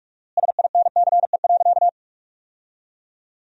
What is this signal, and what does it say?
Morse code, heard as a keyed tone, about 46 wpm, 700 Hz. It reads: SINCE.
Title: 4x Word-Spacing - Sentences from Top 300 Words - 45wpm